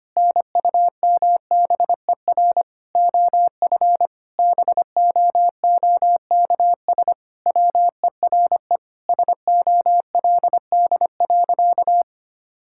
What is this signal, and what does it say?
Morse code, heard as a keyed tone, about 25 wpm, 700 Hz. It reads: NUMBER OF BOOKS WERE SOLD.
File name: practice / 25wpm / Sentences with 4-Word Phrases - 25wpm